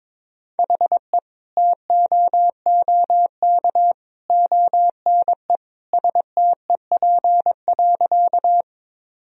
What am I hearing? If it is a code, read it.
Morse code, heard as a keyed tone, about 22 wpm, 700 Hz. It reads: HE TOOK ONE STEP.
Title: Sentences from Top 1000 Words - Review Part 2 - 22wpm